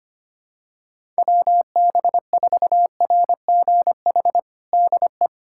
Morse code, heard as a keyed tone, about 25 wpm, 700 Hz. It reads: WB4RGH DE
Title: USA Callsign DE Callsign 25wpm